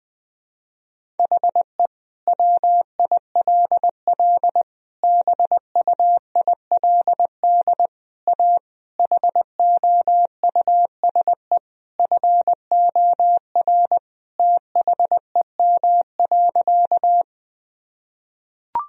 Morse code, heard as a keyed tone, about 21 wpm, 700 Hz. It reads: HE WILL BUILD A HOUSE FOR THEM.
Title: Sentences from Top 1000 Words - Review Part 1 - 20wpm